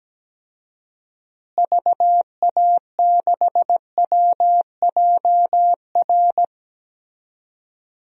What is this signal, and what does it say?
Morse code, heard as a keyed tone, about 17 wpm, 700 Hz. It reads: VA6WJR
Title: DXCC-100 Call Signs - 6 Characters - Repeated 2x - 17wpm